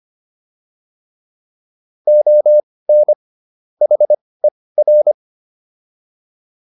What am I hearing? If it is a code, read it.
Morse code, heard as a keyed tone, about 25 wpm, spaced as for 18 wpm, 600 Hz. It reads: ON HER
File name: WD104 18